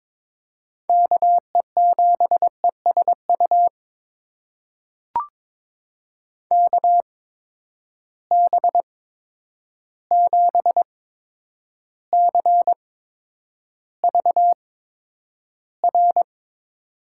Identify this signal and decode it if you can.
Morse code, heard as a keyed tone, about 22 wpm, spaced as for 9 wpm, 700 Hz. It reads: KE7ESU KB7CVR
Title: USA Call Signs - 6 Characters - 8x Character Spacing - 22wpm